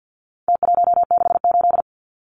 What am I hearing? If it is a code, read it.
Morse code, heard as a keyed tone, about 49 wpm, 700 Hz. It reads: T168
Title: Numbers - Contest Style - 4-Digit 50wpm